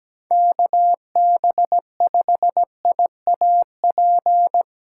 Morse code, heard as a keyed tone, about 17 wpm, 700 Hz. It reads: KB5IAP
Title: USA Callsign DE Callsign - 6 Characters - 17wpm